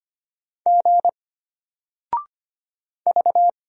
Morse code, heard as a keyed tone, about 25 wpm, 700 Hz. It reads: G V
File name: Single-Letter-Number-25wpm